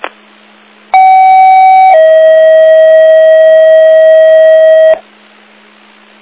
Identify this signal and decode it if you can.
Two-tone page, heard as a one-second tone, then a three-second tone, about 760 then 640 Hz. It is heard from 0.9 to 4.9 s.